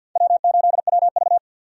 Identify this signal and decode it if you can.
Morse code, heard as a keyed tone, about 52 wpm, 700 Hz. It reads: W8WV
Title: USA Call Signs - 4 Characters - 50wpm